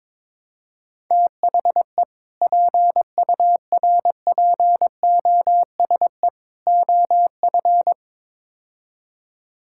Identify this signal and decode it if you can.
Morse code, heard as a keyed tone, about 21 wpm, 700 Hz. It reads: THE PURPOSE OF
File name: practice / 22wpm / Build A Phrase - 5-Word Phrases - 22wpm